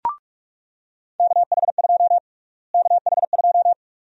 Morse code, heard as a keyed tone, about 44 wpm, 700 Hz. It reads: KH2 KH2
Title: USA Call Sign Prefixes - Repeated 2x - 45wpm